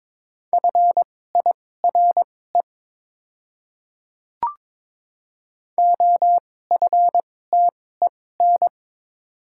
Morse code, heard as a keyed tone, about 22 wpm, spaced as for 16 wpm, 700 Hz. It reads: FIRE OFTEN
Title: 2x Character Spacing - Top 500 Words - 22wpm